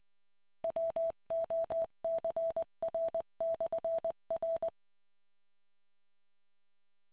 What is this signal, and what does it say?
Morse code, heard as a keyed tone, about 20 wpm, 660 Hz. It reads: WOCR/R